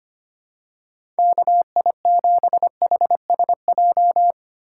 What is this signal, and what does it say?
Morse code, heard as a keyed tone, about 25 wpm, 700 Hz. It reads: KI7HSJ
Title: USA Call Signs - 6 Characters - 6x Character Spacing - 25wpm